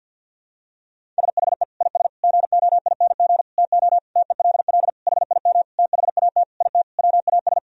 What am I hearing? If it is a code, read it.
Morse code, heard as a keyed tone, about 47 wpm, 700 Hz. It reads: SHE IS GOING TO TELL HIM THAT IT WAS
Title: Sentences from Top 1000 Words - Review Part 1 - 50wpm